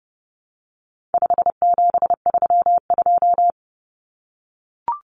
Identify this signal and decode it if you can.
Morse code, heard as a keyed tone, about 30 wpm, 700 Hz. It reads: 5732
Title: Numbers - 4-Digit 30wpm